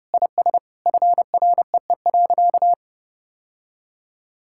Morse code, heard as a keyed tone, about 30 wpm, 700 Hz. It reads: IS FREE.
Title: Sentences from Top 1000 Words - Review Part 6 - 30wpm